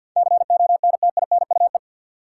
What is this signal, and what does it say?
Morse code, heard as a keyed tone, about 52 wpm, 700 Hz. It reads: CONTINUE
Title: Sentences from Top 1000 Words - Review Part 6 - New Word Drill - 50wpm